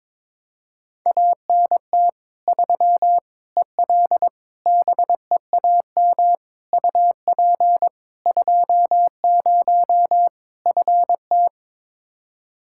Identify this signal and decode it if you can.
Morse code, heard as a keyed tone, about 22 wpm, 700 Hz. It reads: ANT 3 EL BEAM UP 20 FT
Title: Practice QSO - Exchange 2 - Info - 22wpm